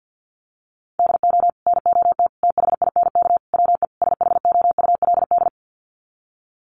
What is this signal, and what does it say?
Morse code, heard as a keyed tone, about 50 wpm, 700 Hz. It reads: DO NOT THINK WE SHOULD
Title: Sets of 5 Words Encore 50wpm